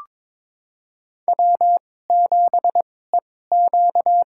Morse code, heard as a keyed tone, about 22 wpm, spaced as for 16 wpm, 700 Hz. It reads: W7EQ
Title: USA Call Signs - 2x Character Spacing - 22wpm